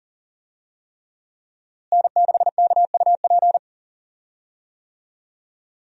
Morse code, heard as a keyed tone, about 40 wpm, 700 Hz. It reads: N6KUP
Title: USA Call Signs - 5 Characters - Repeated 3x - 40wpm